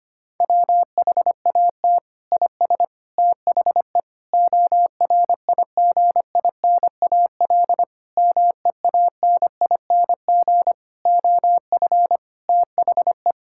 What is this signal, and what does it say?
Morse code, heard as a keyed tone, about 25 wpm, 700 Hz. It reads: WHAT IS THE ORIGINAL MEANING OF THE
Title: Sentences from Top 1000 Words - Review Part 6 - 25wpm